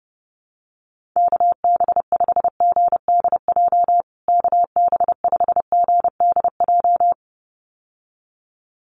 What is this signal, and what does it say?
Morse code, heard as a keyed tone, about 30 wpm, 700 Hz. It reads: KB5GDJ KB5GDJ
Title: USA Call Signs Repeated 2x 30wpm